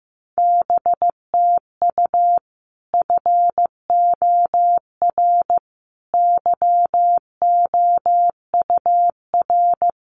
Morse code, heard as a keyed tone, about 15 wpm, 700 Hz. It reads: BTU FOR YOUR
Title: Common QSO Phrases 15wpm